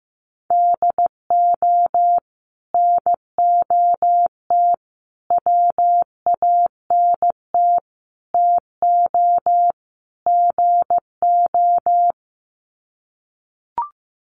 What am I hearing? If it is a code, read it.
Morse code, heard as a keyed tone, about 15 wpm, 700 Hz. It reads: DO NOT WANT TO GO
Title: Sets of 5 Words 15wpm